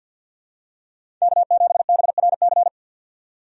Morse code, heard as a keyed tone, about 50 wpm, 700 Hz. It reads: K7BRC
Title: USA Call Signs - 5 Characters - 50wpm